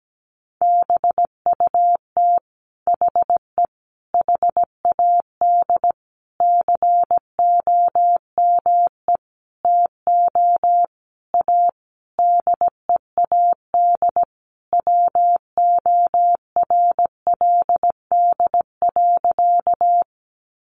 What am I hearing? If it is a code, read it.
Morse code, heard as a keyed tone, about 17 wpm, 700 Hz. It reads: BUT HE HAD COME TO A DEAD WORLD.
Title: Sentences from Top 1500 Words 17wpm